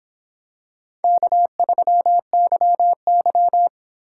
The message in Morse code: K3YY

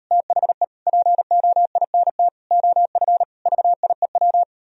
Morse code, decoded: THE POINT OF VIEW

